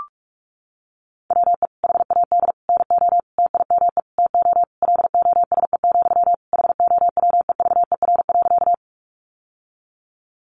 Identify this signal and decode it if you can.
Morse code, heard as a keyed tone, about 44 wpm, 700 Hz. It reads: WE HAD NO TIME TO LOSE, HOWEVER.
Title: Sentences from Top 1100 Words 45wpm